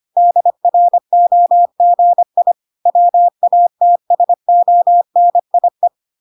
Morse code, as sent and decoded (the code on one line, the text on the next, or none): -.. .-. --- --. .. / .-- .- - ... --- -. .. .
DROGI WATSONIE